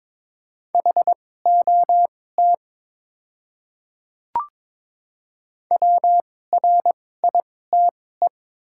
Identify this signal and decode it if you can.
Morse code, heard as a keyed tone, about 22 wpm, spaced as for 16 wpm, 700 Hz. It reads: HOT WRITE
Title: 2x Character Spacing - Top 200 Words - 22wpm